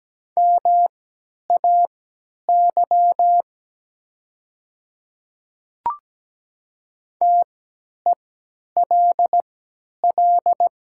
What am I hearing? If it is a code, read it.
Morse code, heard as a keyed tone, about 17 wpm, spaced as for 10 wpm, 700 Hz. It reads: MAY TELL
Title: 3x Character Spacing - Top 200 Words - 17wpm